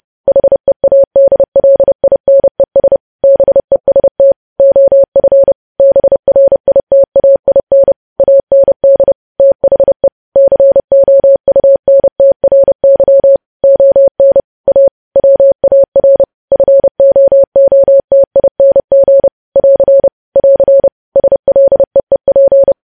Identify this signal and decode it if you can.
Morse code, heard as a keyed tone, about 30 wpm, 550 Hz. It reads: HEADLINES BEST OF BRITAIN AND THE COUNTRY ON A WAR FOOTING <AR> <AR> SLEEP
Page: News Headlines in Morse Code at 30 WPM